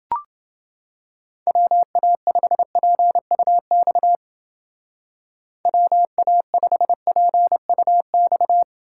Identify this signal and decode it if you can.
Morse code, heard as a keyed tone, about 30 wpm, 700 Hz. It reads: WA5PUX WA5PUX